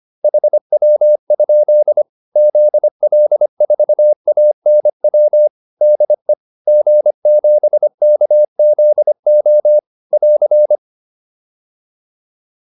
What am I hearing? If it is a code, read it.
Morse code, heard as a keyed tone, about 25 wpm, 600 Hz. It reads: HW? ZL4ANW DE G7KZO <AR>
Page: intermediate-practice-files